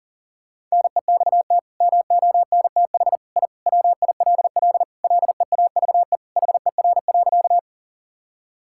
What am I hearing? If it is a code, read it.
Morse code, heard as a keyed tone, about 40 wpm, 700 Hz. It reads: NEXT MONTH I WILL LEAVE HER.
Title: Sentences from Top 1000 Words - Review Part 5 - 40wpm